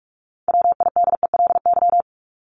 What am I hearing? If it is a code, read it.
Morse code, heard as a keyed tone, about 44 wpm, 700 Hz. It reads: WIDELY
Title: LY Suffix - Top 100 Words with Base Words 45wpm